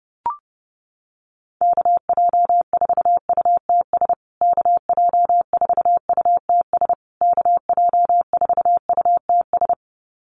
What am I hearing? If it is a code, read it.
Morse code, heard as a keyed tone, about 30 wpm, 700 Hz. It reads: KJ4UTS KJ4UTS KJ4UTS
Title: USA Call Signs Repeated 3x 30wpm